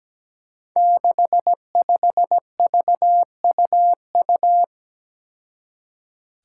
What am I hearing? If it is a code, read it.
Morse code, heard as a keyed tone, about 17 wpm, 700 Hz. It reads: 65VUU
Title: Five Challenging Letter-Number Combinations 17wpm